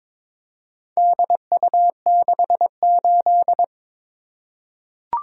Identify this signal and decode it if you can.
Morse code, heard as a keyed tone, about 22 wpm, 700 Hz. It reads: DU68
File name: practice / 22wpm / Four Letter-Number Combinations - 22wpm